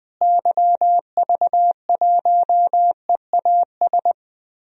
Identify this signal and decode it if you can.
Morse code, heard as a keyed tone, about 20 wpm, 700 Hz. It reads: YV1EAS
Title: DXCC-100 Call Signs - 6 Characters - Repeated 2x - 20wpm